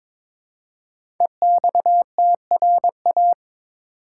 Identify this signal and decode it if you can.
Morse code, heard as a keyed tone, about 22 wpm, 700 Hz. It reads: EXTRA
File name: 5 Letter Words - First 3 Letters Given - 22wpm